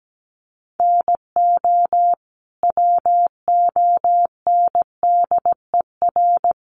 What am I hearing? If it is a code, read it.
Morse code, heard as a keyed tone, about 17 wpm, 700 Hz. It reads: NO WONDER
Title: Sentences from Top 1100 Words 17wpm